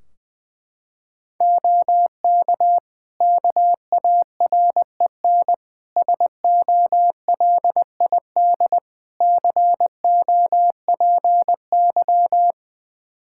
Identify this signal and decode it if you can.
Morse code, heard as a keyed tone, about 20 wpm, 700 Hz. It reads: OK KAREN SOLID COPY
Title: Practice QSO - Exchange 3 - More Info - 20wpm